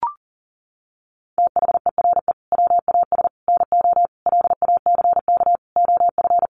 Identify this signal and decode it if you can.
Morse code, heard as a keyed tone, about 40 wpm, 700 Hz. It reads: THERE WAS NO LACK OF
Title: Sentences from Top 1100 Words 40wpm